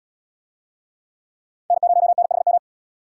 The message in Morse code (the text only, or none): N0NHM